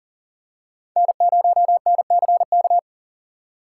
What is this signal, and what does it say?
Morse code, heard as a keyed tone, about 40 wpm, 700 Hz. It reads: N0NCK
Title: USA Call Signs - 5 Characters - 2x Character Spacing - 40wpm